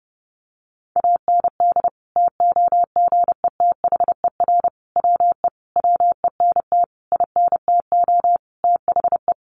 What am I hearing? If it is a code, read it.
Morse code, heard as a keyed tone, about 30 wpm, 700 Hz. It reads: AND TOGETHER WE WENT INTO THE
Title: Sentences from Top 600 Words 30wpm